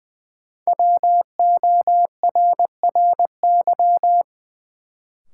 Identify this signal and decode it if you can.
Morse code, heard as a keyed tone, about 20 wpm, 700 Hz. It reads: WORRY
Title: Sentences from Top 1100 Words - New Word Drill - 20wpm